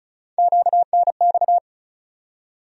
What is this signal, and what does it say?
Morse code, heard as a keyed tone, about 36 wpm, 700 Hz. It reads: QNX
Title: USA Call Sign Postfixes - 3 Letters 35wpm